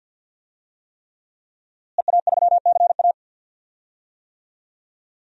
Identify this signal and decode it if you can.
Morse code, heard as a keyed tone, about 52 wpm, 700 Hz. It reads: EA3CA
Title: DXCC-100 Call Signs - 5 Characters - Repeated 3x - 50wpm